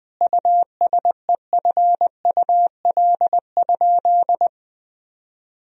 Morse code, heard as a keyed tone, about 20 wpm, 700 Hz. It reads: USEFUL?
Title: Sentences from Top 1000 Words - Review Part 4 - 20wpm